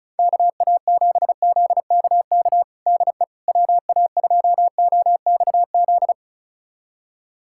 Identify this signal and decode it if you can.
Morse code, heard as a keyed tone, about 34 wpm, 700 Hz. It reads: KA7ZKK DE WA2OXZ
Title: USA Callsign DE Callsign - 6 Characters - 35wpm